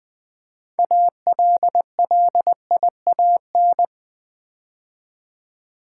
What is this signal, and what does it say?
Morse code, heard as a keyed tone, about 20 wpm, 700 Hz. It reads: ALLIAN
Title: Build A Word - 3 Letters At A Time - 8-Letter Words - 20wpm